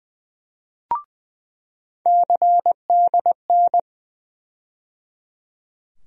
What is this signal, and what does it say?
Morse code, heard as a keyed tone, about 20 wpm, 700 Hz. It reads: CDN
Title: USA Call Sign Postfixes - 3 Letters 20wpm